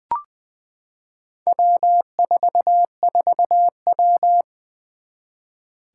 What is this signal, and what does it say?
Morse code, heard as a keyed tone, about 20 wpm, 700 Hz. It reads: W44W
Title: Four Challenging Letter-Number Combinations 20wpm